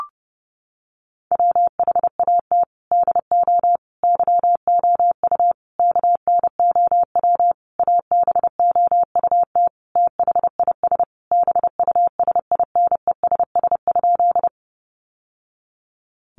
Morse code, heard as a keyed tone, about 30 wpm, 700 Hz. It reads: WHAT DO YOU KNOW ABOUT THIS BUSINESS?
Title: Sentences from Top 600 Words 30wpm